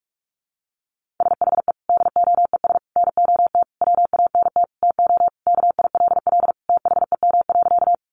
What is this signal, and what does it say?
Morse code, heard as a keyed tone, about 44 wpm, 700 Hz. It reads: SHE DOES NOT WANT TO KILL THEM.
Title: Additional Easy Phrases 45wpm